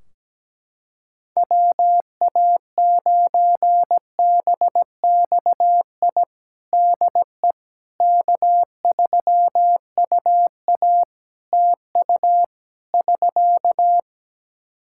Morse code, heard as a keyed tone, about 17 wpm, 700 Hz. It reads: WA9BXI DE K3UA TU <SK>
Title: K1USN SST QSO Practice 17wpm